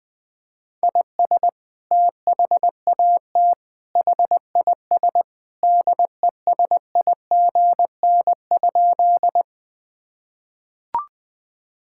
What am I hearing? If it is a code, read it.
Morse code, heard as a keyed tone, about 21 wpm, 700 Hz. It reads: IS THAT HIS DESIGN?
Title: Sentences from Top 1000 Words - Review Part 5 - 20wpm